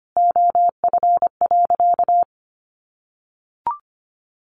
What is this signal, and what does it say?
Morse code, heard as a keyed tone, about 25 wpm, 700 Hz. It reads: OF.
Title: Sentences from Top 1300 Words 25wpm